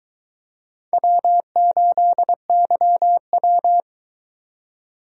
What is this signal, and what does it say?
Morse code, heard as a keyed tone, about 22 wpm, 700 Hz. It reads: W8YW